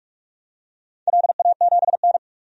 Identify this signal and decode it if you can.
Morse code, heard as a keyed tone, about 45 wpm, 700 Hz. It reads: LA7N
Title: DXCC-100 Call Signs - 4 Characters - 45wpm